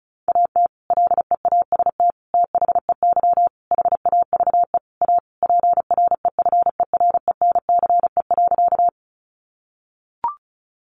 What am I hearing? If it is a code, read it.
Morse code, heard as a keyed tone, about 34 wpm, 700 Hz. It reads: AT LEAST THEY HAVE A PREFERENCE.
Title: Sentences from Top 1500 Words 35wpm